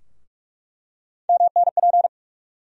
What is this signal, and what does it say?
Morse code, heard as a keyed tone, about 46 wpm, 700 Hz. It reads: MNP
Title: USA Call Sign Postfixes - 3 Letters 45wpm